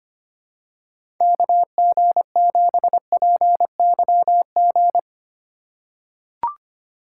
Morse code, heard as a keyed tone, about 25 wpm, 700 Hz. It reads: KG7PYG
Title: USA Call Signs - 3x Character Spacing - 25wpm